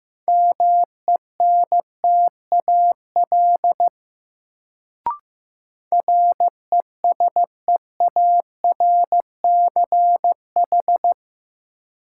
Morse code, heard as a keyed tone, about 15 wpm, 700 Hz. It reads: MENTAL RESEARCH
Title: Sentences from Top 1000 Words - Review Part 4 - New Word Drill - 15wpm